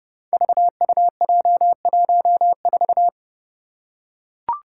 Morse code, heard as a keyed tone, about 32 wpm, 700 Hz. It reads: VUJ14
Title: Five Challenging Letter-Number Combinations 30wpm